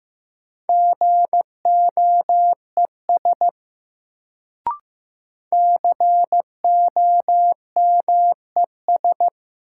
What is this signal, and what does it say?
Morse code, heard as a keyed tone, about 15 wpm, 700 Hz. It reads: GOES COMES
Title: Sentences from Top 1000 Words - Review Part 5 - New Word Drill - 15wpm